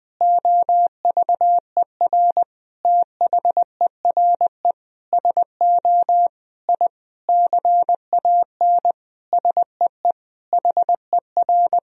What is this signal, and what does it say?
Morse code, heard as a keyed tone, about 20 wpm, 700 Hz. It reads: OVER THERE SO I CAN SEE HER
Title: Sentences from Top 1000 Words - Review Part 3 - 20wpm